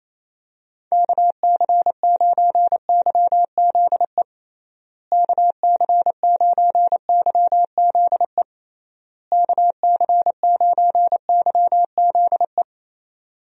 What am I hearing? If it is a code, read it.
Morse code, heard as a keyed tone, about 28 wpm, 700 Hz. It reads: KC9YZE KC9YZE KC9YZE
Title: USA Call Signs - 6 Characters - Repeated 3x - 28wpm